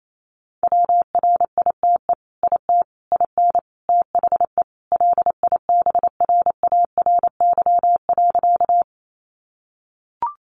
Morse code, heard as a keyed tone, about 28 wpm, 700 Hz. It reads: WRITE IT IN THE LIBRARY.
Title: Sentences from Top 600 Words 28wpm